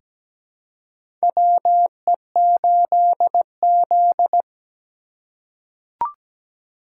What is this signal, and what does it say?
Morse code, heard as a keyed tone, about 17 wpm, 700 Hz. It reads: WE8Z
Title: USA Call Signs - 4 Characters - 2x Character Spacing - 17wpm